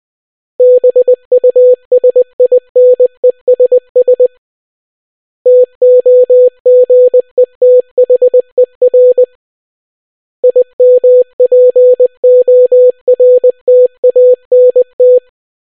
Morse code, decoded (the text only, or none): BUSINESS TOGETHER IMPORTANT